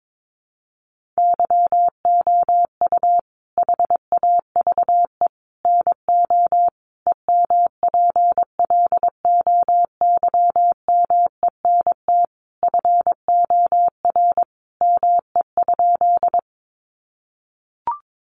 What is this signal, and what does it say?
Morse code, heard as a keyed tone, about 22 wpm, 700 Hz. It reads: YOU HAVE NO EMPLOYMENT FOR ME?
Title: Sentences from Top 1500 Words 22wpm